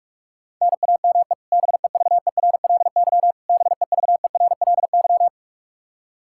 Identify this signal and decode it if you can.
Morse code, heard as a keyed tone, about 46 wpm, 700 Hz. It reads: NAME BEVERLY BEVERLY
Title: Practice QSO - Exchange 1 - Beginning - 45wpm